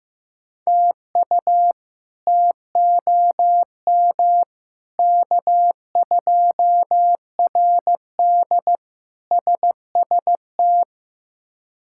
Morse code, heard as a keyed tone, about 15 wpm, 700 Hz. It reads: TU TOM K2RD SST
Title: K1USN SST QSO Practice - 15wpm